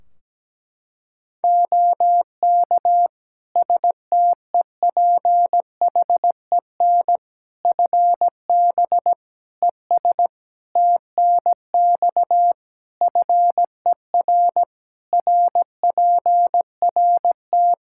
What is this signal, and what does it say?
Morse code, heard as a keyed tone, about 17 wpm, 700 Hz. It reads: OK STEPHEN FB ES TNX FER RPRT
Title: Practice QSO - Exchange 2 - Info - 17wpm